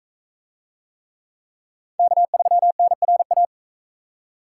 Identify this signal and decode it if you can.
Morse code, heard as a keyed tone, about 39 wpm, 700 Hz. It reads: K3NRA